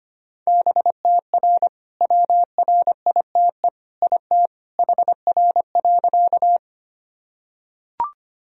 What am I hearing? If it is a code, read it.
Morse code, heard as a keyed tone, about 25 wpm, 700 Hz. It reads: BTR WRITE IT HR.
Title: Sentences from Top 400 Words and Abbreviations 25wpm